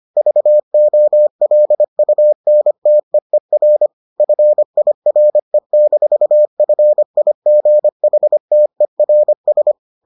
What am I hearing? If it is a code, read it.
Morse code, heard as a keyed tone, about 25 wpm, 600 Hz. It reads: VOLUNTEER FIRE-FIGHTERS